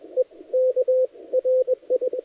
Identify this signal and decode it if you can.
Morse code, heard as a keyed tone, about 20 wpm, spaced as for 17 wpm, 510 Hz, with radio noise. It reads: EKRI
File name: dlazaawansowanych / nagraniazpasma / titawaNETcontrol